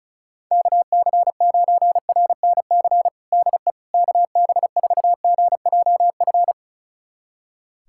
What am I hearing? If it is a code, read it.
Morse code, heard as a keyed tone, about 36 wpm, 700 Hz. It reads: KC9RNC DE KB4GJF
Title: USA Callsign DE Callsign 35wpm